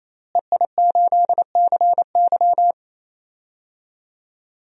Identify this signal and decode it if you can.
Morse code, heard as a keyed tone, about 28 wpm, 700 Hz. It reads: EI8CY
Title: DXCC-100 Call Signs - 5 Characters - Repeated 3x - 28wpm